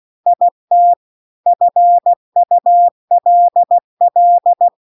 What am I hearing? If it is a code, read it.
Morse code, heard as a keyed tone, about 16 wpm, 700 Hz. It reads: IT FULL